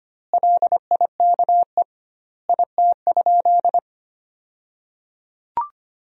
Morse code, heard as a keyed tone, about 25 wpm, 700 Hz. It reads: LIKE IT?
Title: 2x Word-Spacing - Sentences from Top 300 Words - 25wpm